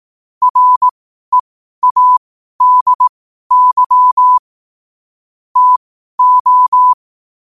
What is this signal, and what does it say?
Morse code, heard as a keyed tone, about 18 wpm, spaced as for 12 wpm, 1000 Hz. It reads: READY TO
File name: qotdfiles / 12wpm